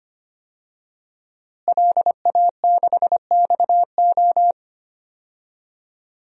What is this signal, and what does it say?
Morse code, heard as a keyed tone, about 25 wpm, 700 Hz. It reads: LA6XO
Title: DXCC-100 Call Signs - 5 Characters - Repeated 3x - 25wpm